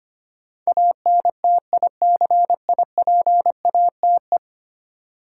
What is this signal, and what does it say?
Morse code, heard as a keyed tone, about 25 wpm, 700 Hz. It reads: ANTICIPATE
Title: Sentences from Top 1500 Words - New Word Drill - 25wpm